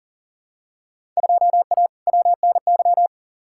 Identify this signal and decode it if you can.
Morse code, heard as a keyed tone, about 40 wpm, 700 Hz. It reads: 2A WNY
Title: Field Day Exchanges - 40wpm